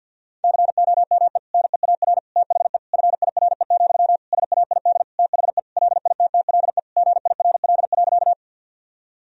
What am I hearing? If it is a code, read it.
Morse code, heard as a keyed tone, about 52 wpm, 700 Hz. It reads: COME NEAR THE FIRE, SAID THE LITTLE GIRL.
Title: Sentences from Top 1000 Words - Review Part 1 - 50wpm